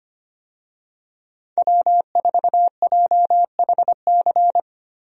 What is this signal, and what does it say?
Morse code, heard as a keyed tone, about 25 wpm, 700 Hz. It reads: W4JHC
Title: USA Call Signs - 5 Characters - 25wpm